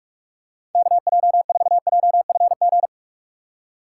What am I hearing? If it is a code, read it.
Morse code, heard as a keyed tone, about 46 wpm, 700 Hz. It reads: KJ4JFG
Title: USA Call Signs - 6 Characters - 2x Character Spacing - 45wpm